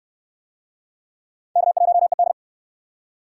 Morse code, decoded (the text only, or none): K1ED